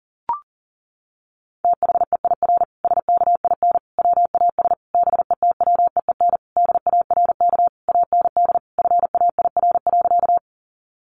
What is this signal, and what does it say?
Morse code, heard as a keyed tone, about 40 wpm, 700 Hz. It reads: THEIR SKIN WAS BETWEEN DARK AND FAIR.
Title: Sentences from Top 1300 Words 40wpm